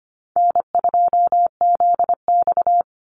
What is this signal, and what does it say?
Morse code, heard as a keyed tone, about 25 wpm, 700 Hz. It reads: N2ZX
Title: Call Signs - 25wpm